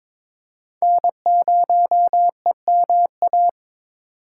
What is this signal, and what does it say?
Morse code, heard as a keyed tone, about 23 wpm, 700 Hz. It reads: N0EMA
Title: USA Call Signs - 5 Characters - 3x Character Spacing - 22wpm